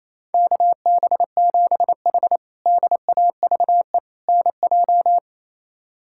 Morse code, heard as a keyed tone, about 28 wpm, 700 Hz. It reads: KB7H DAVE NJ
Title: K1USN SST QSO Practice - 28wpm